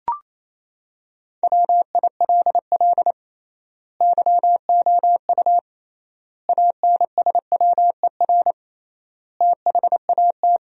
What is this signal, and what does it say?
Morse code, heard as a keyed tone, about 28 wpm, 700 Hz. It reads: WILL YOU ANSWER THAT
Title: 3x Word-Spacing - Sentences from Top 400 Words - 28wpm